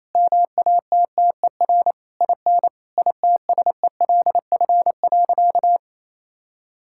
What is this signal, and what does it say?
Morse code, heard as a keyed tone, about 28 wpm, 700 Hz. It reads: MATTER IN ITSELF.
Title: Sentences from Top 1000 Words - Review Part 5 - 28wpm